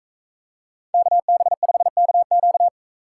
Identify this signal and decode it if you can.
Morse code, heard as a keyed tone, about 42 wpm, 700 Hz. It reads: KB5KQ